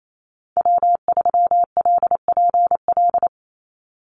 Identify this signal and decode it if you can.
Morse code, heard as a keyed tone, about 28 wpm, 700 Hz. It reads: W3LPL
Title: Call Signs Repeated 3x - 28wpm